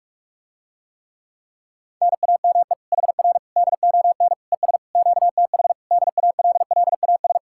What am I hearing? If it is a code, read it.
Morse code, heard as a keyed tone, about 44 wpm, 700 Hz. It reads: NAME HR DON ES QTH DALLAS